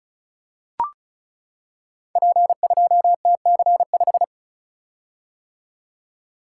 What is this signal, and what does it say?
Morse code, heard as a keyed tone, about 33 wpm, 690 Hz. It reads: P2TC5
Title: Code Groups - 5 Letter:Numbers - 35wpm